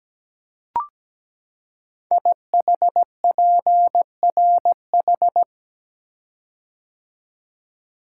Morse code, heard as a keyed tone, about 17 wpm, 700 Hz. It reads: IHPRH
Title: Code Groups - 5 Letters - 17wpm